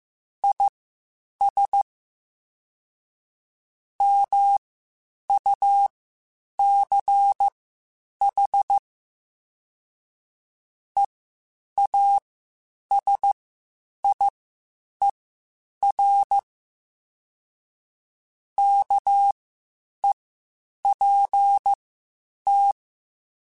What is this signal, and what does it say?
Morse code, heard as a keyed tone, about 15 wpm, spaced as for 8 wpm, 800 Hz. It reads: IS MUCH EASIER KEPT
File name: Quotes Spaced / Quotes Spaced-mp3s / Character sp @13